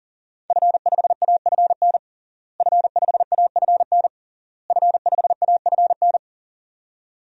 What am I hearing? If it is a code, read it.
Morse code, heard as a keyed tone, about 37 wpm, 700 Hz. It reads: F5AFN F5AFN F5AFN
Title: DXCC-100 Call Signs - 5 Characters - Repeated 3x - 40wpm